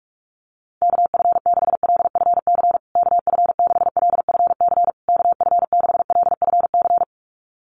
Morse code, heard as a keyed tone, about 44 wpm, 700 Hz. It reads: KF6LFC KF6LFC KF6LFC
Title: USA Call Signs Repeated 3x 45wpm